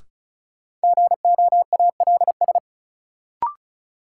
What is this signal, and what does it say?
Morse code, heard as a keyed tone, about 34 wpm, 700 Hz. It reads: GOALS
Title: Sentences from Top 1000 Words - Review Part 6 - New Word Drill - 35wpm